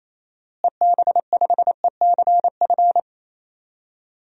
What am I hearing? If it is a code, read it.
Morse code, heard as a keyed tone, about 27 wpm, 700 Hz. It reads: EB5ECF
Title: DXCC-100 Call Signs - 6 Characters - Repeated 2x - 28wpm